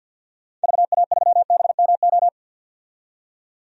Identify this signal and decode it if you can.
Morse code, heard as a keyed tone, about 52 wpm, 700 Hz. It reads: VA3BMO
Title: DXCC-100 Call Signs - 6 Characters - 50wpm